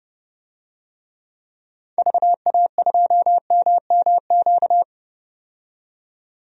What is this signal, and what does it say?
Morse code, heard as a keyed tone, about 29 wpm, 700 Hz. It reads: VA2MMQ
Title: DXCC-100 Call Signs - 6 Characters - 30wpm